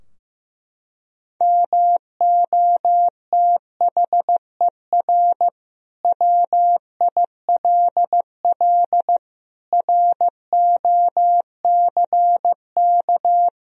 Morse code, heard as a keyed tone, about 15 wpm, 700 Hz. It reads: MOTHER WILL ROCK
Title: Sentences from Top 1000 Words - Review Part 2 - 15wpm